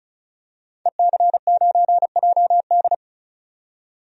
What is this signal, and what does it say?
Morse code, heard as a keyed tone, about 36 wpm, 700 Hz. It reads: EC9JD
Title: DXCC-100 Call Signs - 5 Characters - 35wpm